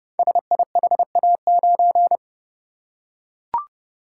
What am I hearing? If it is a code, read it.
Morse code, heard as a keyed tone, about 30 wpm, 690 Hz. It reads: SIHA9
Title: Code Groups - 5 Letter:Numbers - 30wpm